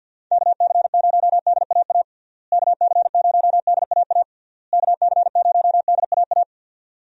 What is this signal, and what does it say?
Morse code, heard as a keyed tone, about 53 wpm, 700 Hz. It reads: KC0DAA KC0DAA KC0DAA
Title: USA Call Signs - 6 Characters - Repeated 3x - 50wpm